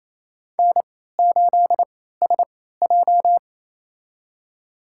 Morse code, heard as a keyed tone, about 28 wpm, spaced as for 16 wpm, 700 Hz. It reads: N8SJ
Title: USA Call Signs - 3x Character Spacing - 28wpm